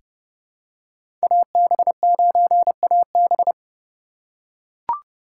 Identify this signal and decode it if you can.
Morse code, heard as a keyed tone, about 32 wpm, 700 Hz. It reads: AB9AB
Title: USA Call Signs - 5 Characters - 3x Character Spacing - 30wpm